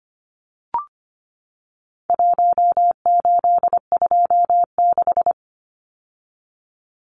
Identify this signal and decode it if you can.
Morse code, heard as a keyed tone, about 25 wpm, 700 Hz. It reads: 1826
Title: Numbers - Contest Style - 4-Digit 25wpm